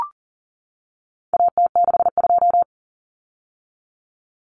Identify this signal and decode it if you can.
Morse code, heard as a keyed tone, about 40 wpm, 700 Hz. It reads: AT62
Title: Common Cut Numbers - 4-Digit 40wpm